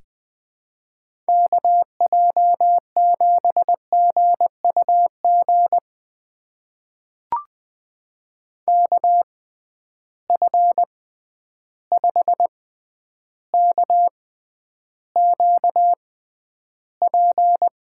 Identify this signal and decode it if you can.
Morse code, heard as a keyed tone, about 20 wpm, spaced as for 10 wpm, 700 Hz. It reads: KJ7GUG KF5KQP
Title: USA Call Signs - 6 Characters - 6x Character Spacing - 20wpm